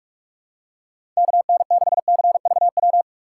Morse code, heard as a keyed tone, about 46 wpm, 700 Hz. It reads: KN6CVW
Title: USA Call Signs - 6 Characters - 4x Character Spacing - 45wpm